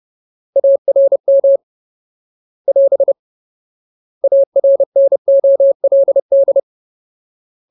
Morse code, heard as a keyed tone, about 30 wpm, 550 Hz. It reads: ARM <AS> ARNOLD
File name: Ultimate Word List / 30WPM Ultimate